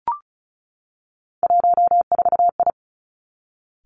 Morse code, heard as a keyed tone, about 36 wpm, 700 Hz. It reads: 14I
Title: Three Challenging Letter-Number Combinations 35wpm